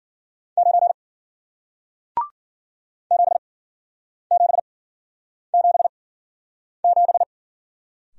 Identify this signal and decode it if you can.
Morse code, heard as a keyed tone, about 51 wpm, spaced as for 9 wpm, 700 Hz. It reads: 9 7777